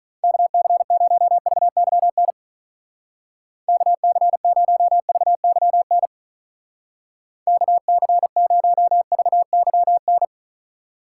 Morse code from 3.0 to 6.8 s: KC0VYN

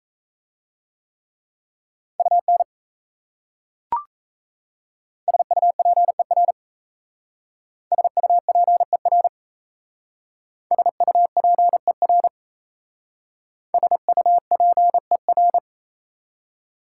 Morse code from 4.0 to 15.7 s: SUPER SUPER SUPER SUPER

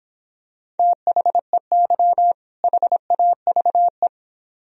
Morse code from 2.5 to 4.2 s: HAVE

